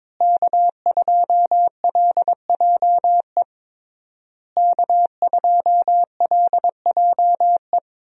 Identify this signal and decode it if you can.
Morse code, heard as a keyed tone, about 22 wpm, 700 Hz. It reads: K2LJE K2LJE
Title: USA Call Signs - 5 Characters - Repeated 3x - 22wpm